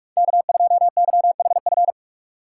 Morse code, heard as a keyed tone, about 46 wpm, 700 Hz. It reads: K2YHF
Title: USA Call Signs - 5 Characters - 2x Character Spacing - 45wpm